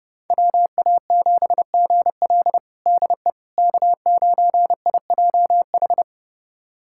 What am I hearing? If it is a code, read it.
Morse code, heard as a keyed tone, about 29 wpm, 700 Hz. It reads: WA7GL DE K9IJH
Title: USA Callsign DE Callsign - 5 Characters - 30wpm